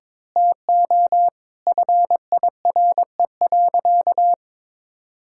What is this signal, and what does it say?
Morse code, heard as a keyed tone, about 22 wpm, 700 Hz. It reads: TO FIRE.
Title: Sentences from Top 1000 Words - Review Part 6 - 22wpm